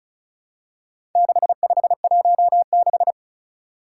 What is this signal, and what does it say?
Morse code, heard as a keyed tone, about 36 wpm, 700 Hz. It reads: 6516
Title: Four Challenging Letter-Number Combinations 35wpm